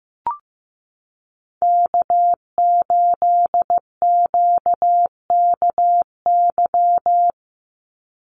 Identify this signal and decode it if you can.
Morse code, heard as a keyed tone, about 15 wpm, 700 Hz. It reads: K8QKY
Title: Call Signs Repeated 2x - 15wpm